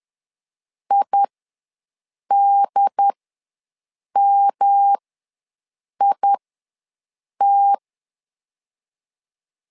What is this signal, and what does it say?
Morse code, heard as a keyed tone, about 11 wpm, spaced as for 6 wpm, 800 Hz. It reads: IDMIT
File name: LEZIONI / 3 MWRO / 3 MWR ws 10wpm 2151